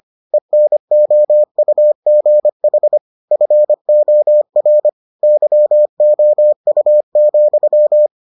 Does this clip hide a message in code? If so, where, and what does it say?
Yes: Morse code from 0.3 to 8.1 s, heard as a keyed tone, about 25 wpm, 600 Hz. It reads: ENOUGH FOR YOU,